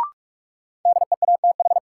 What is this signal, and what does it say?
Morse code, heard as a keyed tone, about 47 wpm, 700 Hz. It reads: DEATH
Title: Sentences from Top 1100 Words - New Word Drill - 45wpm